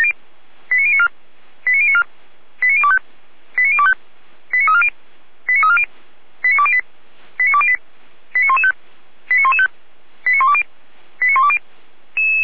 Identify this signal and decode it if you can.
Five-tone selective call (ZVEI), heard as quick runs of five tones, tones of about 70 ms. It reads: 89094 89094 89025 89025 89340 89340 89208 89208 89115 89115 89120 89120